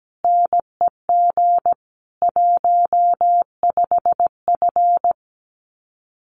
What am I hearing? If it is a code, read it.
Morse code, heard as a keyed tone, about 17 wpm, 700 Hz. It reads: NEG 15F
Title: Common QSO Phrases Encore 17wpm